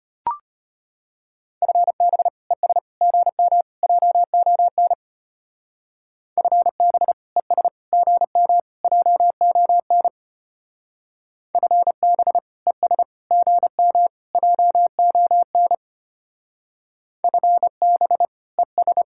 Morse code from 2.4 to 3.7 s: ES GM

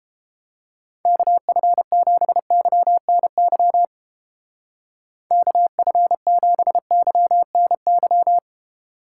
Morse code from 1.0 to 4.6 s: KF7YNY

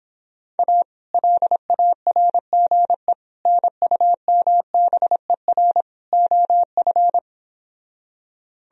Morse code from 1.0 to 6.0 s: LARGE NUMBER

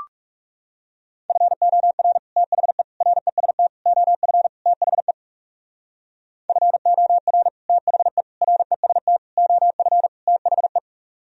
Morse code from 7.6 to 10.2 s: THE REST OF

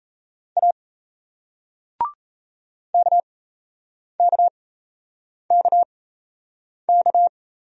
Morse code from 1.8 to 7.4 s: KKKK

